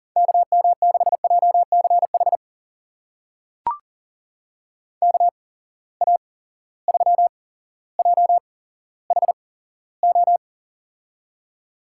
Morse code, heard as a keyed tone, about 40 wpm, spaced as for 17 wpm, 700 Hz. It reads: KM6JCH KA3JHO
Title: USA Call Signs - 6 Characters - 8x Character Spacing - 40wpm